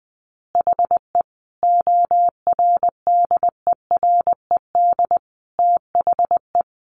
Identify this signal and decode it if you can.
Morse code, heard as a keyed tone, about 20 wpm, 700 Hz. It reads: HE ORDERED THE
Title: Additional Easy Phrases 20wpm